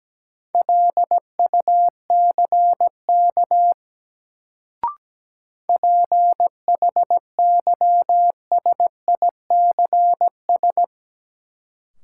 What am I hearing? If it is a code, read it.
Morse code, heard as a keyed tone, about 17 wpm, 700 Hz. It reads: LUCK PHYSICS
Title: Sentences from Top 1500 Words - New Word Drill Alt - 17wpm